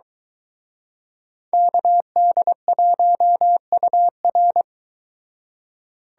Morse code, heard as a keyed tone, about 23 wpm, 700 Hz. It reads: KD1UR